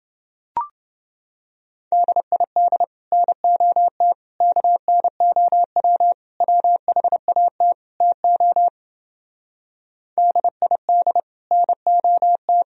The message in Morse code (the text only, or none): DID NOT KNOW WHAT TO DID NOT